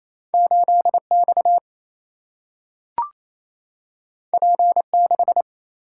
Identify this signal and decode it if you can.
Morse code, heard as a keyed tone, about 28 wpm, 700 Hz. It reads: 8X P6
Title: Single-Letter Single-Number 28wpm